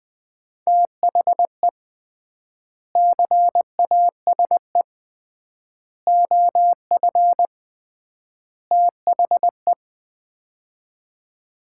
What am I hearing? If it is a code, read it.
Morse code, heard as a keyed tone, about 20 wpm, 700 Hz. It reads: THE CASE OF THE
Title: 3x Word-Spacing - Sets of 4 Words - 20wpm